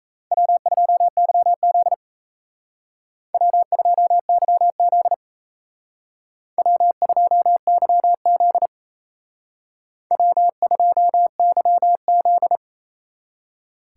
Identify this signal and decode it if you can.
Morse code, heard as a keyed tone, about 36 wpm, 700 Hz. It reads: W2YZ W2YZ W2YZ W2YZ